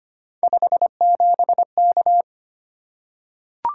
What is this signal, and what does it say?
Morse code, heard as a keyed tone, about 25 wpm, 700 Hz. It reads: 57K
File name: Three Letter-Number Combinations - 25wpm